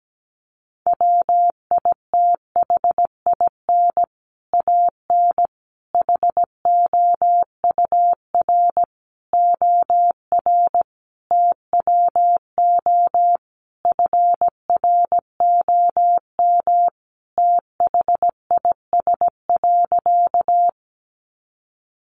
Morse code, heard as a keyed tone, about 17 wpm, 700 Hz. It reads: WITHIN AN HOUR OR TWO FROM THIS.
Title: Sentences from Top 700 Words 17wpm